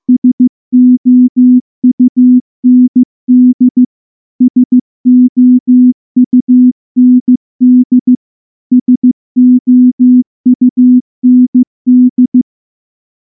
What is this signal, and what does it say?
Morse code, heard as a keyed tone, about 14 wpm, 250 Hz. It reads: SOUND SOUND SOUND
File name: morse-training-15WPM